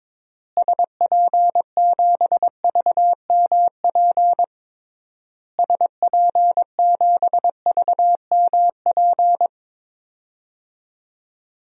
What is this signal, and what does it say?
Morse code, heard as a keyed tone, about 22 wpm, 700 Hz. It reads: SP7VMP SP7VMP
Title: DXCC-100 Call Signs - 6 Characters - Repeated 3x - 22wpm